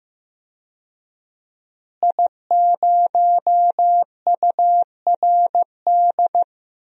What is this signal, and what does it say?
Morse code, heard as a keyed tone, about 15 wpm, 700 Hz. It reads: I0URD
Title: DXCC-100 Call Signs - 5 Characters - Repeated 3x - 15wpm